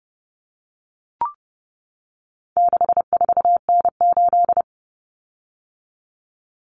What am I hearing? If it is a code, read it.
Morse code, heard as a keyed tone, about 30 wpm, 700 Hz. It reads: 64N8
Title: Common Cut Numbers - 4-Digit 30wpm